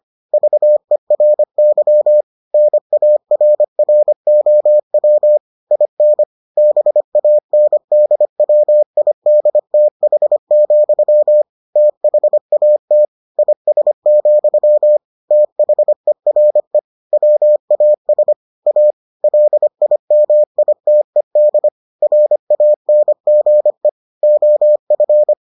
Morse code, heard as a keyed tone, about 25 wpm, 600 Hz. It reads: VERY NARROW IN BANDWIDTH, THAT IS, THERE WAS A LIMITED RANGE OF